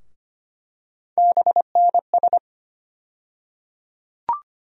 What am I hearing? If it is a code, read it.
Morse code, heard as a keyed tone, about 25 wpm, 700 Hz. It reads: BNS
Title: USA Call Sign Postfixes - 3 Letters 25wpm